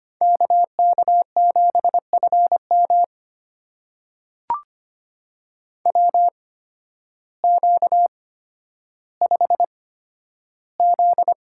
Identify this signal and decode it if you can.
Morse code, heard as a keyed tone, about 25 wpm, spaced as for 12 wpm, 700 Hz. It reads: KK7FM WQ5Z
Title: USA Call Signs - 8x Character Spacing - 25wpm